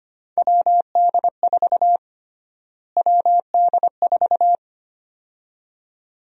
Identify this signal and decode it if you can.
Morse code, heard as a keyed tone, about 25 wpm, 700 Hz. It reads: WD4 WD4
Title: USA Call Sign Prefixes - Repeated 2x - 25wpm